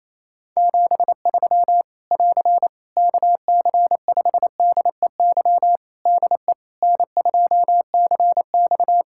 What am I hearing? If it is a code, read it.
Morse code, heard as a keyed tone, about 28 wpm, 700 Hz. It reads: 73 <AR> KC5DEY DE N2CX
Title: Practice QSO - Exchange 4 - Ending - 28wpm